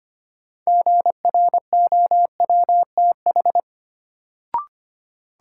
Morse code, heard as a keyed tone, about 25 wpm, 700 Hz. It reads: GROWTH
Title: Sentences from Top 1000 Words - Review Part 5 - New Word Drill - 25wpm